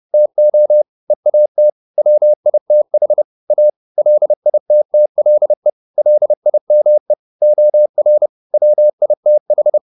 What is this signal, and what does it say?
Morse code, heard as a keyed tone, about 30 wpm, 600 Hz. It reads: TO EAT WITH A LITTLE LIME OR WITH
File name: the old man and the sea / the old man and the sea 1 30wpm